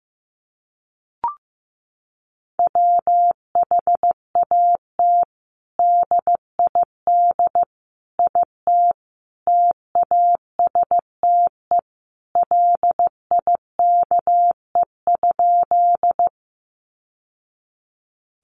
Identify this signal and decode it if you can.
Morse code, heard as a keyed tone, about 15 wpm, 700 Hz. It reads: WHAT DID IT TASTE LIKE?
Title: Sentences from Top 1100 Words 15wpm